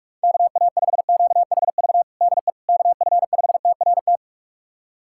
Morse code, heard as a keyed tone, about 46 wpm, 700 Hz. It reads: KA5QHV DE KF5TRT
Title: USA Callsign DE Callsign - 6 Characters - 45wpm